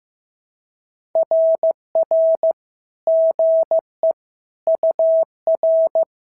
Morse code, heard as a keyed tone, about 15 wpm, 650 Hz. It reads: RR GE UR
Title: SOTA QSO Practice 15wpm